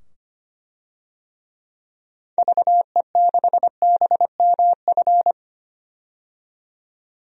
Morse code, heard as a keyed tone, about 25 wpm, 700 Hz. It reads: VE6BMF
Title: DXCC-100 Call Signs - 6 Characters - 25wpm